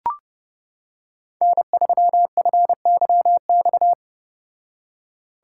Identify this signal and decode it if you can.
Morse code, heard as a keyed tone, about 29 wpm, 700 Hz. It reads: N3FYX